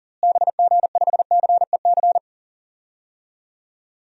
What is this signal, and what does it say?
Morse code, heard as a keyed tone, about 38 wpm, 700 Hz. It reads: BG5CEC
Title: DXCC-100 Call Signs - 6 Characters - Repeated 3x - 40wpm